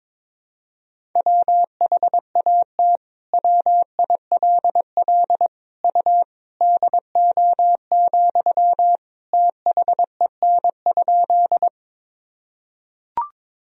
Morse code, heard as a keyed tone, about 22 wpm, 700 Hz. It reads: WHAT WILL U DO, THEN?
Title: Sentences from Top 100 Words and Abbreviations 22wpm